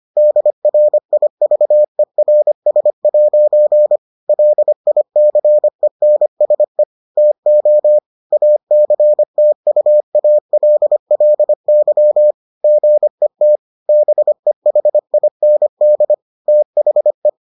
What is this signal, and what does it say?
Morse code, heard as a keyed tone, about 25 wpm, 600 Hz. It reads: DRIVERS' LICENSE TO ACTUALLY GET BEHIND THE